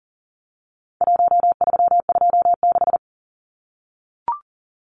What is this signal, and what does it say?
Morse code, heard as a keyed tone, about 40 wpm, 700 Hz. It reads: 1326
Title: Numbers - Contest Style - 4-Digit 40wpm